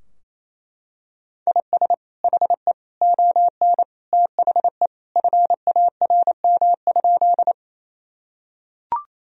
Morse code, heard as a keyed tone, about 27 wpm, 700 Hz. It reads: IS HE ON THE FARM?
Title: Sentences from Top 1000 Words - Review Part 2 - 28wpm